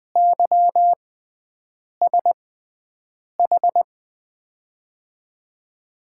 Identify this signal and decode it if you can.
Morse code, heard as a keyed tone, about 20 wpm, spaced as for 7 wpm, 700 Hz. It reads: YSH
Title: USA Call Signs - 6x Character Spacing - 20wpm